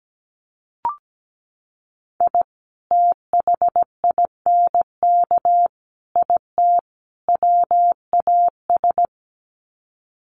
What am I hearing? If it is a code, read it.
Morse code, heard as a keyed tone, about 17 wpm, 700 Hz. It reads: I THINK IT WAS
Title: Sets of 4 Words 17wpm